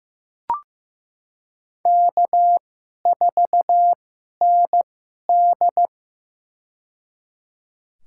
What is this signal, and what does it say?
Morse code, heard as a keyed tone, about 16 wpm, spaced as for 11 wpm, 700 Hz. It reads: K4ND
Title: USA Call Signs - 2x Character Spacing - 15wpm